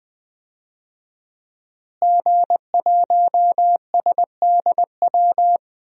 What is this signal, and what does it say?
Morse code, heard as a keyed tone, about 20 wpm, 700 Hz. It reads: G1SDW
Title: DXCC-100 Call Signs - 5 Characters - Repeated 3x - 20wpm